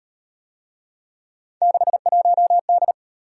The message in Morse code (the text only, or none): 61D